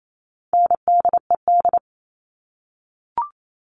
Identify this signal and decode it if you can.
Morse code, heard as a keyed tone, about 28 wpm, 700 Hz. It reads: NDED
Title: All Cut Numbers - 4-Digit 28wpm